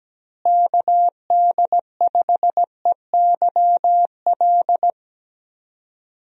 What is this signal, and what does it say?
Morse code, heard as a keyed tone, about 17 wpm, 700 Hz. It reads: KD5EYL
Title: USA Call Signs - 6 Characters - Repeated 3x - 17wpm